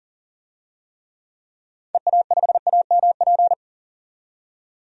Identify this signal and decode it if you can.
Morse code, heard as a keyed tone, about 40 wpm, 700 Hz. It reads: EA5AMP
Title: DXCC-100 Call Signs - 6 Characters - 40wpm